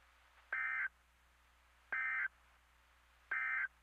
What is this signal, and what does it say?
Emergency Alert System (SAME) tones, heard as harsh buzzing bursts. It is heard from 0.4 to 3.8 s.
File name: NOAA Weather Radio MKE-KEC60 Weekly Test